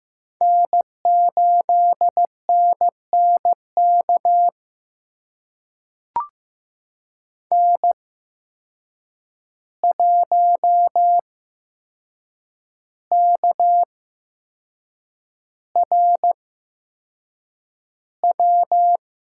Morse code, heard as a keyed tone, about 15 wpm, spaced as for 6 wpm, 700 Hz. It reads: N8NNK N1KRW
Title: USA Call Signs - 5 Characters - 8x Character Spacing - 15wpm